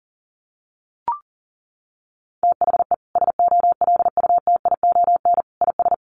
Morse code, heard as a keyed tone, about 40 wpm, 700 Hz. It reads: THE SOLUTION IS
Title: Sentences from Top 1300 Words 40wpm